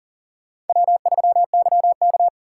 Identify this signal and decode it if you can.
Morse code, heard as a keyed tone, about 50 wpm, 700 Hz. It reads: W3YK